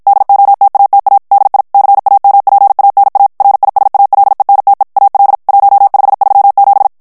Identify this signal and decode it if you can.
Morse code, heard as a keyed tone, about 50 wpm, 800 Hz. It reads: DOTATA DI CAMPANA RISALENTE AL 1537
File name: morse mp3 / ITALIEN texte clair 1 vitesse 50 mots minute